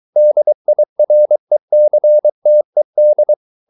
Morse code, heard as a keyed tone, about 23 wpm, 600 Hz. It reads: DIRECTED